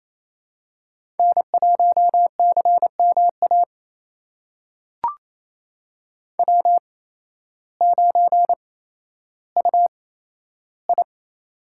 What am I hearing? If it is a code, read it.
Morse code, heard as a keyed tone, about 28 wpm, spaced as for 13 wpm, 700 Hz. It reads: N1CMA W9UI
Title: USA Call Signs - 8x Character Spacing - 28wpm